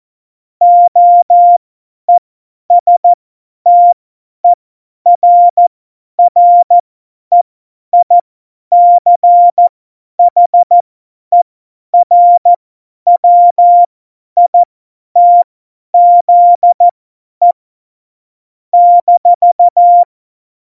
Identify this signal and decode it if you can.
Morse code, heard as a keyed tone, about 14 wpm, spaced as for 10 wpm, 700 Hz. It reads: OESTERREICHERWITZE -
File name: Morsekurs / Witze / Witz 24. November 2020, 50 F70